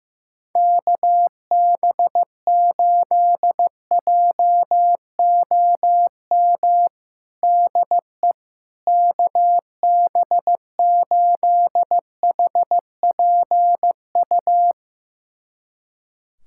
Morse code, heard as a keyed tone, about 15 wpm, 700 Hz. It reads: KB8JOM DE KB8HPU
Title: USA Callsign DE Callsign - 6 Characters - 15wpm